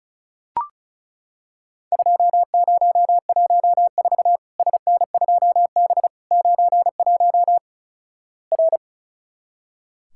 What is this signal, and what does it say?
Morse code, heard as a keyed tone, about 36 wpm, 700 Hz. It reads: 2014 SN2B 91 R
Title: Worked All Europe DX Contest - QTC Exchange - 35wpm